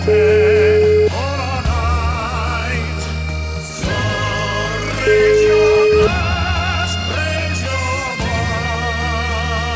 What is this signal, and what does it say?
Ringback tone (Europe, 425 Hz), with noise.